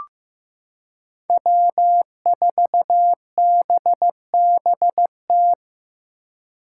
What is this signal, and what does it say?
Morse code, heard as a keyed tone, about 16 wpm, 700 Hz. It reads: W4BBT
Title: K1USN SST QSO Practice - 15wpm